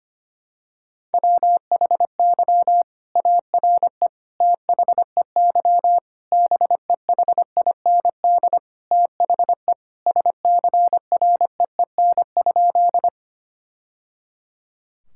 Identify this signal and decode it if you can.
Morse code, heard as a keyed tone, about 25 wpm, 700 Hz. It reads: WHY ARE THEY BEHIND THE SCREEN?
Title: Sentences from Top 1000 Words - Review Part 6 - 25wpm